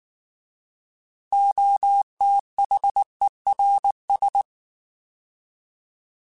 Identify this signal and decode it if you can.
Morse code, heard as a keyed tone, about 19 wpm, 800 Hz. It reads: OTHERS
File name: Alphabetical / Alphabetical mp3s / Os @18